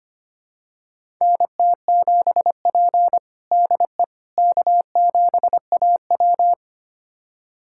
Morse code, heard as a keyed tone, about 25 wpm, 700 Hz. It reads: NT7P DE K7AW
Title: USA Callsign DE Callsign - 4 Characters - 25wpm